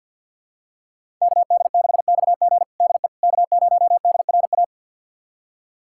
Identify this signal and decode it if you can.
Morse code, heard as a keyed tone, about 51 wpm, 700 Hz. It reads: KD6XG DE K0DRA
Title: USA Callsign DE Callsign - 5 Characters - 50wpm